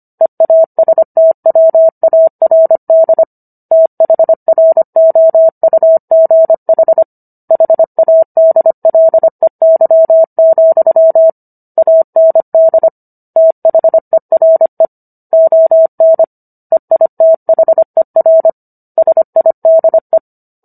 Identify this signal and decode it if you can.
Morse code, heard as a keyed tone, about 25 wpm, 650 Hz. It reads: EASTWARD THROUGH HADLEY, AND THERE ON EITHER SIDE